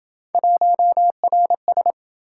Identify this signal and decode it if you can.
Morse code, heard as a keyed tone, about 27 wpm, 700 Hz. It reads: 1RS